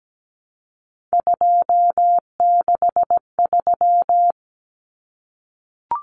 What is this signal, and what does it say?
Morse code, heard as a keyed tone, about 17 wpm, 700 Hz. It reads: 263
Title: Numbers - 3-Digit 17wpm